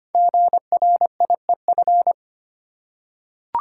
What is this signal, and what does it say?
Morse code, heard as a keyed tone, about 25 wpm, 700 Hz. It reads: GRIEF
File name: 5 Letter Words - First 3 Letters Given - 25wpm